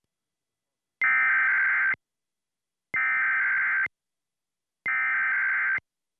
Emergency Alert System (SAME) tones (header bursts), heard as harsh buzzing bursts. It is heard from 1.0 to 5.8 s.